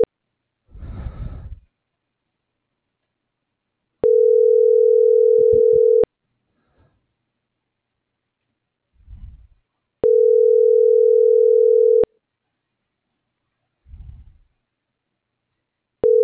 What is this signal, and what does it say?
Ringback tone (North America), 440 + 480 Hz.